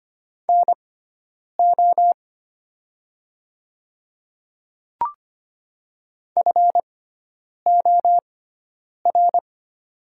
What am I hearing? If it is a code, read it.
Morse code, heard as a keyed tone, about 25 wpm, spaced as for 9 wpm, 700 Hz. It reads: NO FOR
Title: 6x Character Spacing - Top 100 Words - 25wpm